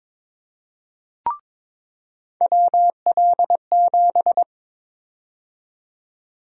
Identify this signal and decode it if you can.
Morse code, heard as a keyed tone, about 22 wpm, 700 Hz. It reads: WL7
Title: USA Call Sign Prefixes - 22wpm